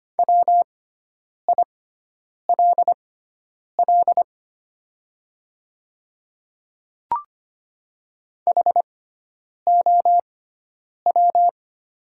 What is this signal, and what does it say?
Morse code, heard as a keyed tone, about 25 wpm, spaced as for 9 wpm, 700 Hz. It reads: WILL HOW
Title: 6x Character Spacing - Top 100 Words - 25wpm